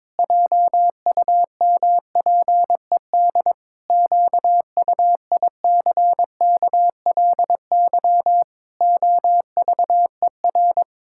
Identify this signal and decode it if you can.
Morse code, heard as a keyed tone, about 22 wpm, 700 Hz. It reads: JUMPED QUICKLY OVER
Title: Pangram Challenge v2 - 22wpm